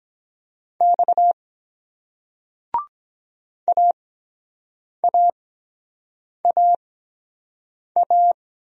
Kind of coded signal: Morse code

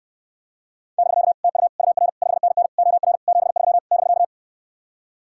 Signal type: Morse code

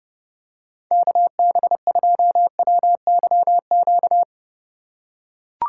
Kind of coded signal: Morse code

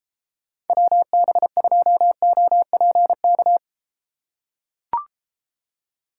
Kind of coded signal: Morse code

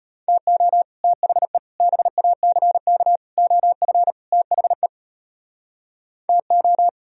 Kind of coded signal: Morse code

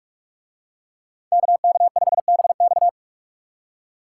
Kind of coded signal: Morse code

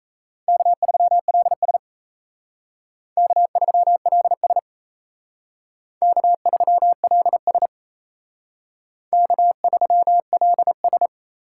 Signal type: Morse code

